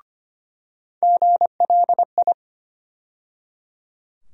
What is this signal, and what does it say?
Morse code, heard as a keyed tone, about 25 wpm, 700 Hz. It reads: GLI
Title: USA Call Sign Postfixes - 3 Letters 25wpm